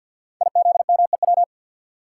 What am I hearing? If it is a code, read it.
Morse code, heard as a keyed tone, about 52 wpm, 700 Hz. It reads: IZMEW
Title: Code Groups - 5 Letters - 50wpm